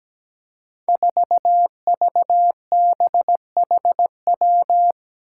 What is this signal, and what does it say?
Morse code, heard as a keyed tone, about 17 wpm, 700 Hz. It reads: 4VBHW
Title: Five Challenging Letter-Number Combinations 17wpm